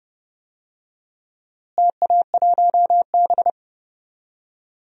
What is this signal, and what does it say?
Morse code, heard as a keyed tone, about 30 wpm, 700 Hz. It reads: TA1B
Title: DXCC-100 Call Signs - 4 Characters - Repeated 3x - 30wpm